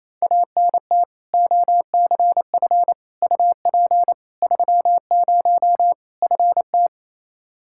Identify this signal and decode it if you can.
Morse code, heard as a keyed tone, about 28 wpm, 700 Hz. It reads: ANT OCF UP 30 FT
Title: Practice QSO - Exchange 2 - Info - 28wpm